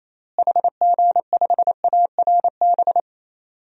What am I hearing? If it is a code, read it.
Morse code, heard as a keyed tone, about 28 wpm, 700 Hz. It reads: HG5ARB
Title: DXCC-100 Call Signs - 6 Characters - Repeated 2x - 28wpm